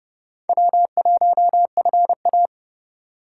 Morse code, heard as a keyed tone, about 31 wpm, 700 Hz. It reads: W1FA
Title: USA Call Signs - 4 Characters - 4x Character Spacing - 30wpm